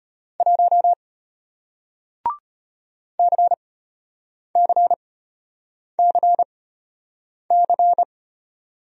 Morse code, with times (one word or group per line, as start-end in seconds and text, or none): 0.4-0.9 1
3.2-8.0 CCCC